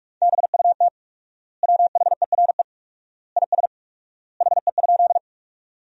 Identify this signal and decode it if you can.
Morse code, heard as a keyed tone, about 41 wpm, 700 Hz. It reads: BUT WHERE IS HE?
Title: 4x Word-Spacing - Sentences from Top 200 Words - 45wpm